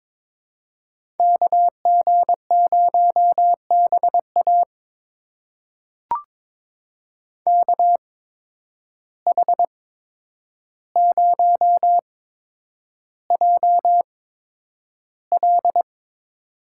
Morse code, heard as a keyed tone, about 22 wpm, spaced as for 9 wpm, 700 Hz. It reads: KG0BA KH0JL
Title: USA Call Signs - 5 Characters - 8x Character Spacing - 22wpm